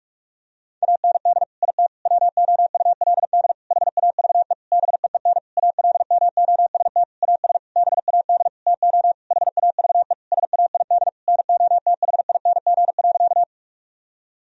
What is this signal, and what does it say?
Morse code, heard as a keyed tone, about 46 wpm, 700 Hz. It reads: AND IT WOULD HAVE BEEN ALMOST AS BAD TO HAVE SAID NOTHING.
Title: Sentences from Top 1000 Words - Review Part 4 - 45wpm